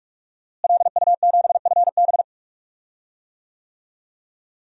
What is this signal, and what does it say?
Morse code, heard as a keyed tone, about 43 wpm, 700 Hz. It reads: LU7FB
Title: DXCC-100 Call Signs - 5 Characters - Repeated 3x - 45wpm